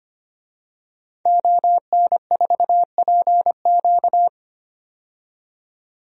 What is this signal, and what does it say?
Morse code, heard as a keyed tone, about 25 wpm, 700 Hz. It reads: ON4PQ
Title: DXCC-100 Call Signs - 5 Characters - 25wpm